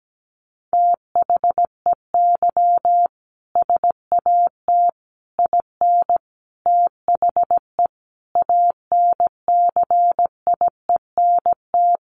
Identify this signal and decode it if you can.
Morse code, heard as a keyed tone, about 17 wpm, 700 Hz. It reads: THEY SAT IN THE ANCIENT
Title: A Sound Of Thunder 17wpm